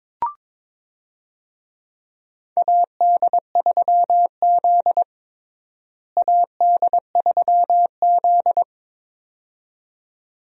Morse code, heard as a keyed tone, about 22 wpm, 700 Hz. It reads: AD3Z AD3Z
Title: USA Call Signs - 4 Characters - Repeated 2x - 22wpm